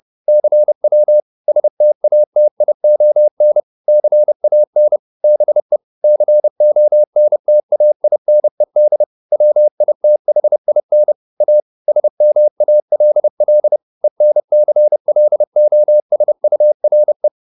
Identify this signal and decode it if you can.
Morse code, heard as a keyed tone, about 30 wpm, 600 Hz. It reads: CW STATION CAN BE CONTAINED WITHIN A SMALL ENCLOSURE